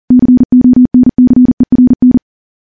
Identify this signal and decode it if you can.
Morse code, heard as a keyed tone, about 40 wpm, 250 Hz. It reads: CONCERN
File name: morse / morse-training-40WPM